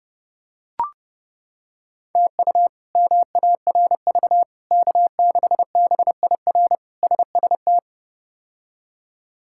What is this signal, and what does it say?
Morse code, heard as a keyed tone, about 30 wpm, 700 Hz. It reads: TU MARV K6BIR SST
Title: K1USN SST QSO Practice - 30wpm